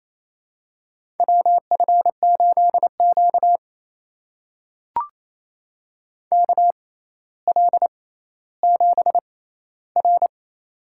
Morse code, heard as a keyed tone, about 29 wpm, spaced as for 15 wpm, 700 Hz. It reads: WF8Q KL7R
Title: USA Call Signs - 4 Characters - 6x Character Spacing - 28wpm